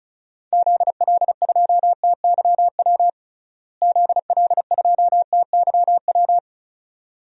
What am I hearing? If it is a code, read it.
Morse code, heard as a keyed tone, about 36 wpm, 700 Hz. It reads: ZL2TYW ZL2TYW
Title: DXCC-100 Call Signs - 6 Characters - Repeated 2x - 35wpm